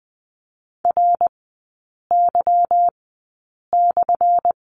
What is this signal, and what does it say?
Morse code, heard as a keyed tone, about 20 wpm, spaced as for 8 wpm, 690 Hz. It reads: RY/
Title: 198 Lesson 32e - Letter k - Three Characters 20wpm